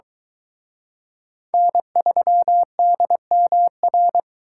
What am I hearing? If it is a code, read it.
Morse code, heard as a keyed tone, about 23 wpm, 700 Hz. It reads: N3DMR